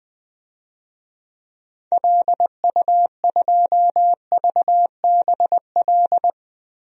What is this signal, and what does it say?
Morse code, heard as a keyed tone, about 21 wpm, 700 Hz. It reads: LU2VBL
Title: DXCC-100 Call Signs - 6 Characters - 20wpm